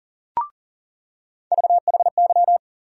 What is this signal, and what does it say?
Morse code, heard as a keyed tone, about 40 wpm, 700 Hz. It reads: VHY
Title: USA Call Sign Postfixes - 3 Letters 40wpm